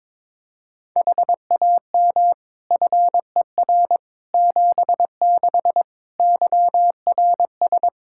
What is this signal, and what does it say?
Morse code, heard as a keyed tone, about 21 wpm, 700 Hz. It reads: HAM FER 76 YRS
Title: Practice QSO - Exchange 3 - More Info - 22wpm